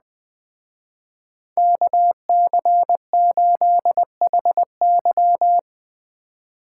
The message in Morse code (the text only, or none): KC8HY